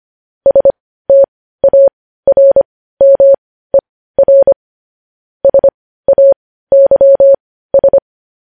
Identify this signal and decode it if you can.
Morse code, heard as a keyed tone, about 25 wpm, spaced as for 15 wpm, 550 Hz. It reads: STARMER SAYS
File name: podcast / News headlines-1x-25x15